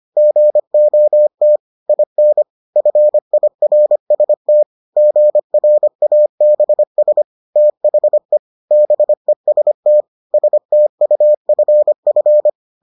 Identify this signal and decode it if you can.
Morse code, heard as a keyed tone, about 25 wpm, 600 Hz. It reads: GOT IN FIRST GRABS THE BEST STUFF